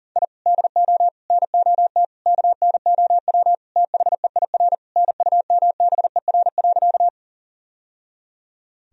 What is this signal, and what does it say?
Morse code, heard as a keyed tone, about 40 wpm, 700 Hz. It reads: I DO NOT KNOW THEIR NUMBER.
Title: Sentences from Top 1000 Words - Review Part 1 - 40wpm